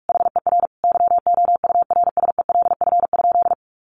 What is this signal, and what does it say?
Morse code, heard as a keyed tone, about 44 wpm, 700 Hz. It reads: HER YOURSELF?
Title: Sentences from Top 1500 Words 45wpm